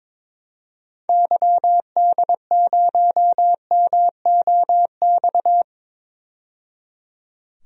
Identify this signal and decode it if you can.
Morse code, heard as a keyed tone, about 21 wpm, 700 Hz. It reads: YD0MOX
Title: DXCC-100 Call Signs - 6 Characters - 22wpm